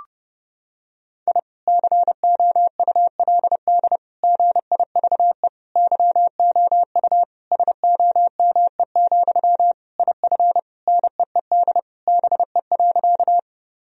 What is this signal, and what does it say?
Morse code, heard as a keyed tone, about 30 wpm, 700 Hz. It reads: I COULD GIVE YOU SOME, IF NEED BE.
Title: Sentences from Top 1000 Words - Review Part 1 - 30wpm